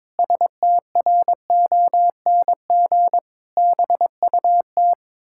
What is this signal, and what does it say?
Morse code, heard as a keyed tone, about 22 wpm, 700 Hz. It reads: STRONG BUT
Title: Pangram Challenge v2 - 22wpm